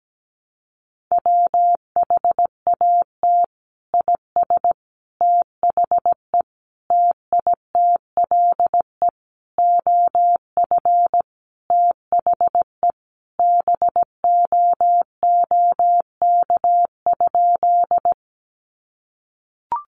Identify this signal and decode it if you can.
Morse code, heard as a keyed tone, about 17 wpm, 700 Hz. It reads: WHAT IS THE TITLE OF THE BOOK?
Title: Sentences from Top 1200 Words 17wpm